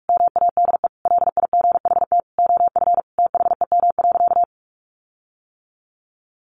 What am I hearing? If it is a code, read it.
Morse code, heard as a keyed tone, about 44 wpm, 700 Hz. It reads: MADE LIGHT OF THEM.
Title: Sentences from Top 1300 Words 45wpm